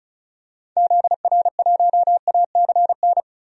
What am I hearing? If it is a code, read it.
Morse code, heard as a keyed tone, about 36 wpm, 700 Hz. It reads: ZR1ACN
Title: DXCC-100 Call Signs - 6 Characters - Repeated 3x - 35wpm